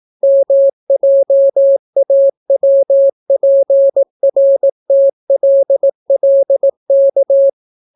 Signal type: Morse code